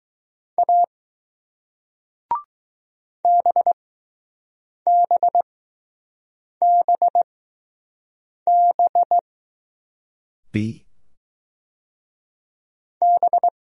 Morse code, heard as a keyed tone, about 20 wpm, spaced as for 6 wpm, 700 Hz. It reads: A BBBB B